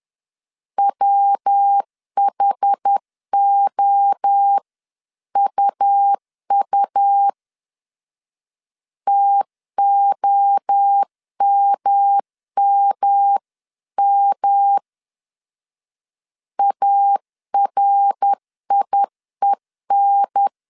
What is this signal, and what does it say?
Morse code, heard as a keyed tone, about 10 wpm, 800 Hz. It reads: WHOUU TOMMM ARIEN
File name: 10wpm